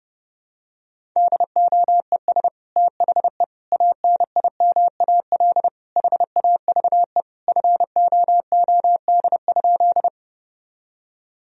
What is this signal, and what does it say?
Morse code, heard as a keyed tone, about 30 wpm, 700 Hz. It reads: DOES THE ANIMAL HAVE FOOD?
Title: Sentences from Top 1000 Words - Review Part 1 - 30wpm